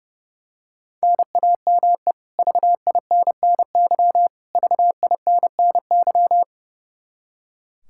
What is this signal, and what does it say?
Morse code, heard as a keyed tone, about 30 wpm, 700 Hz. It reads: NAME VINNY VINNY
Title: K1USN SST QSO Practice 30wpm